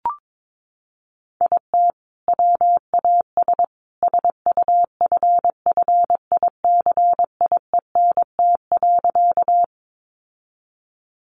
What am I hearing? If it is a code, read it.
Morse code, heard as a keyed tone, about 22 wpm, 700 Hz. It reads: IT WAS SUFFICIENT.
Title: Sentences from Top 1400 Words 22wpm